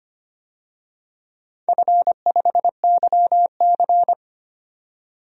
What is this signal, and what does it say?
Morse code, heard as a keyed tone, about 25 wpm, 690 Hz. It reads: F5YC
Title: DXCC-100 Call Signs - 4 Characters - 25wpm